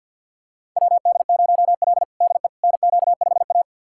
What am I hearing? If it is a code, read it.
Morse code, heard as a keyed tone, about 52 wpm, 700 Hz. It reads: WD0L DE NQ5A
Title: USA Callsign DE Callsign - 4 Characters - 50wpm